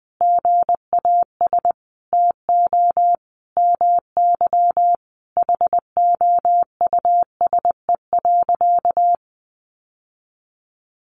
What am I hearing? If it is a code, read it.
Morse code, heard as a keyed tone, about 20 wpm, 700 Hz. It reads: GAS TO MY HOUSE.
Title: Sentences from Top 1100 Words 20wpm